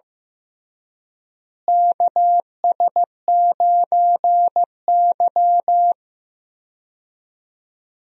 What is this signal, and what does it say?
Morse code, heard as a keyed tone, about 15 wpm, 700 Hz. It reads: KS9Y
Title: USA Call Signs - 4 Characters - Repeated 3x - 15wpm